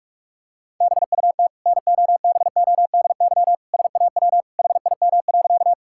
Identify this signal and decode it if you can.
Morse code, heard as a keyed tone, about 42 wpm, 700 Hz. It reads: BUT NOBODY SAW HIM.
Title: Sentences from Top 1600 Words 45wpm